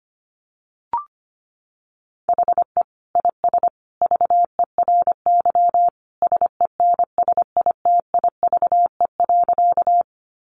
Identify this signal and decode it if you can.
Morse code, heard as a keyed tone, about 25 wpm, 700 Hz. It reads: HE IS VERY SENSITIVE.
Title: Sentences from Top 1400 Words 25wpm